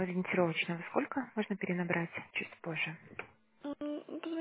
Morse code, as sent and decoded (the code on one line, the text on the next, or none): none